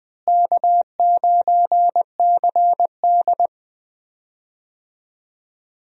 Morse code, heard as a keyed tone, about 20 wpm, 700 Hz. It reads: K9CD
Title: USA Call Signs - 4 Characters - Repeated 2x - 20wpm